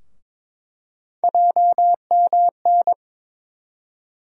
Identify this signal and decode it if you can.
Morse code, heard as a keyed tone, about 22 wpm, 700 Hz. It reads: JMN
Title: USA Call Sign Postfixes - 3 Letters 22wpm